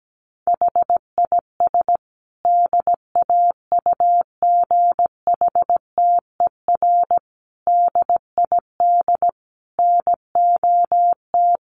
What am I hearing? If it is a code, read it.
Morse code, heard as a keyed tone, about 17 wpm, 700 Hz. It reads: HIS DAUGHTER DID NOT
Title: Sentences from Top 1500 Words 17wpm